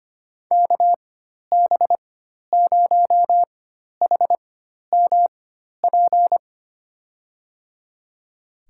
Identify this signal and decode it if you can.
Morse code, heard as a keyed tone, about 25 wpm, spaced as for 12 wpm, 700 Hz. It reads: KB0HMP
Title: USA Call Signs - 6 Characters - 4x Character Spacing - 25wpm